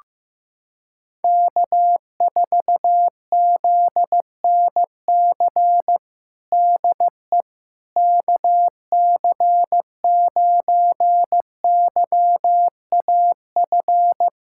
Morse code, heard as a keyed tone, about 16 wpm, 700 Hz. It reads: K4ZNC DE KC9YAF
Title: USA Callsign DE Callsign 15wpm